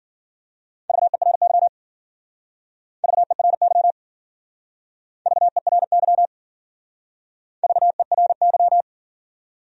Morse code, heard as a keyed tone, about 50 wpm, 700 Hz. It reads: VERY VERY VERY VERY